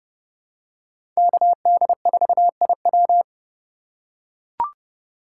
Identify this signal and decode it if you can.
Morse code, heard as a keyed tone, about 29 wpm, 700 Hz. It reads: KD4IW
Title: USA Call Signs - 5 Characters - 2x Character Spacing - 30wpm